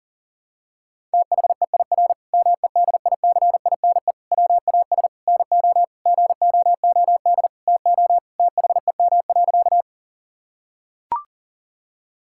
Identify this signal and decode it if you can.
Morse code, heard as a keyed tone, about 37 wpm, 700 Hz. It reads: THEIR MEDICINE WAS NO GOOD TO THEM.
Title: Sentences from Top 1000 Words - Review Part 6 - 40wpm